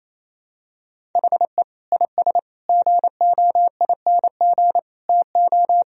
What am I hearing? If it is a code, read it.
Morse code, heard as a keyed tone, about 28 wpm, 700 Hz. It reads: HE IS GOING TO
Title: Sentences from Top 1000 Words - Review Part 4 - 28wpm